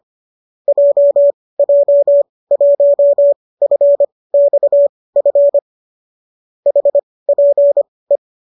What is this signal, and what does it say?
Morse code, heard as a keyed tone, about 25 wpm, spaced as for 18 wpm, 570 Hz. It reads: JJ1FXF HPE